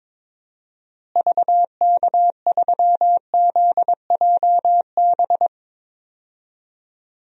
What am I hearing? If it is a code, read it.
Morse code, heard as a keyed tone, about 22 wpm, 700 Hz. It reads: VK3ZJB
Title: DXCC-100 Call Signs - 6 Characters - Repeated 2x - 22wpm